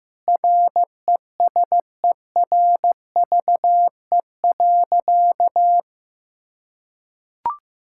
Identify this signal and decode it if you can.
Morse code, heard as a keyed tone, about 15 wpm, 700 Hz. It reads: RESERVE.
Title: Sentences from Top 1000 Words - Review Part 6 - 15wpm